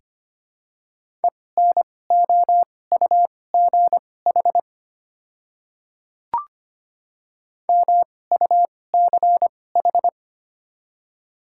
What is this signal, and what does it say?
Morse code, heard as a keyed tone, about 25 wpm, spaced as for 18 wpm, 700 Hz. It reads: ENOUGH MUCH
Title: 2x Character Spacing - Top 500 Words - 25wpm